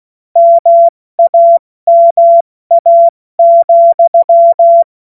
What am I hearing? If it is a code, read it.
Morse code, heard as a keyed tone, about 16 wpm, 680 Hz. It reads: MAMA,
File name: Morsekurs / Witze / Witz 21. September 2021, 70 F80